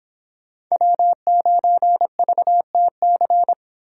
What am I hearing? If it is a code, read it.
Morse code, heard as a keyed tone, about 29 wpm, 700 Hz. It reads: W9VTC